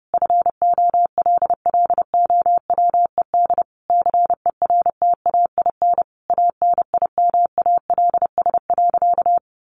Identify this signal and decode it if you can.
Morse code, heard as a keyed tone, about 30 wpm, 700 Hz. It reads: FOLLOWED CERTAIN ANIMALS.
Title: A Sound Of Thunder 30wpm